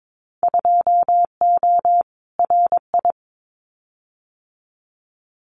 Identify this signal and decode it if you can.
Morse code, heard as a keyed tone, about 22 wpm, 700 Hz. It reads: 2O RI
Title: Winter Field Day Exchanges 22wpm